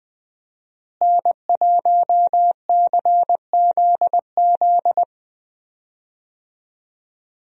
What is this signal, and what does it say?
Morse code, heard as a keyed tone, about 20 wpm, 700 Hz. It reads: N1CZZ
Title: USA Call Signs - 5 Characters - Repeated 3x - 20wpm